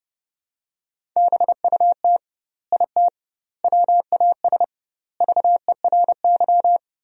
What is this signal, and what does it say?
Morse code, heard as a keyed tone, about 32 wpm, 700 Hz. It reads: BUT IT WAS VERY
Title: 2x Word-Spacing - Sentences from Top 300 Words - 30wpm